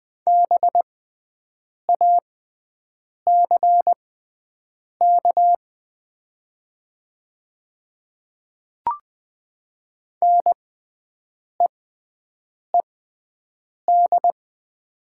Morse code, heard as a keyed tone, about 20 wpm, spaced as for 7 wpm, 700 Hz. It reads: BACK NEED
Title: 6x Character Spacing - Top 200 Words - 20wpm